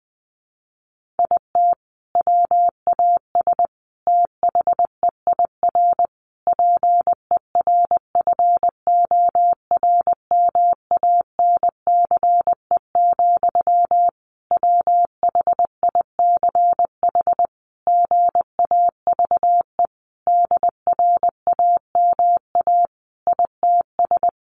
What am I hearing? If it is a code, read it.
Morse code, heard as a keyed tone, about 20 wpm, 700 Hz. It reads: IT WAS THEIR PERFORMANCE, WHICH GAVE DRAMA ITS
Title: Sentences from Top 1500 Words 20wpm